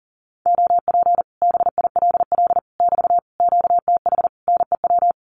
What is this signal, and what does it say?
Morse code, heard as a keyed tone, about 40 wpm, 700 Hz. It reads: OP BILL <BT> QTH NEW
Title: Common QSO Phrases Encore 40wpm